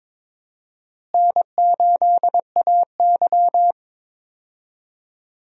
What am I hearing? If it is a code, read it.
Morse code, heard as a keyed tone, about 22 wpm, 700 Hz. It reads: N8AY
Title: USA Call Signs - 4 Characters - Repeated 3x - 22wpm